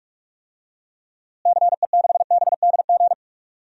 Morse code, heard as a keyed tone, about 42 wpm, 700 Hz. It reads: CE6BDG
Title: DXCC-100 Call Signs - 6 Characters - Repeated 3x - 45wpm